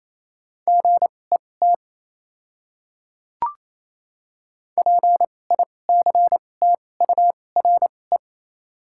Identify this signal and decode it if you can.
Morse code, heard as a keyed tone, about 28 wpm, spaced as for 20 wpm, 700 Hz. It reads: GET PICTURE
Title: 2x Character Spacing - Top 200 Words - 28wpm